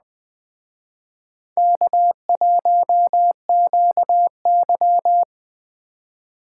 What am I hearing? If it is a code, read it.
Morse code, heard as a keyed tone, about 21 wpm, 700 Hz. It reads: K1QY
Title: Speed Racing - USA 1x2 Call Signs 17wpm